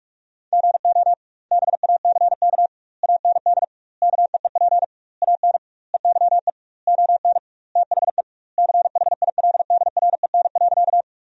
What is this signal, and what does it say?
Morse code, heard as a keyed tone, about 46 wpm, 700 Hz. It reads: GO BACK AND KEEP AN EYE ON THE CHILDREN.
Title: 2x Word-Spacing - Sentences from Top 300 Words - 45wpm